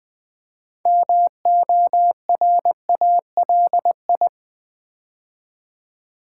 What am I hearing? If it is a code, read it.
Morse code, heard as a keyed tone, about 20 wpm, 700 Hz. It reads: MORALI
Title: Build A Word - 3 Letters At A Time - 8-Letter Words - 20wpm